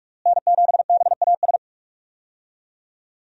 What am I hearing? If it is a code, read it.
Morse code, heard as a keyed tone, about 46 wpm, 700 Hz. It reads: N7BAS
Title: USA Call Signs - 5 Characters - Repeated 2x - 45wpm